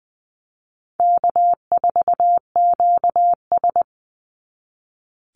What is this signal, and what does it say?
Morse code, heard as a keyed tone, about 20 wpm, 700 Hz. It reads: K4QS
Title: NA SOTA Chaser Callsigns 20wpm